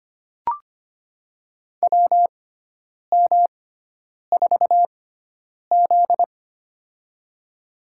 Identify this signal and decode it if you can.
Morse code, heard as a keyed tone, about 25 wpm, spaced as for 9 wpm, 700 Hz. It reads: WM4Z
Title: USA Call Signs - 4 Characters - 6x Character Spacing - 25wpm